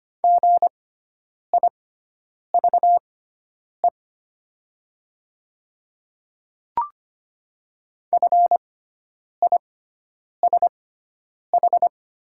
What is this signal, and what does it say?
Morse code, heard as a keyed tone, about 25 wpm, spaced as for 9 wpm, 700 Hz. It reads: GIVE FISH
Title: 6x Character Spacing - Top 300 Words - 25wpm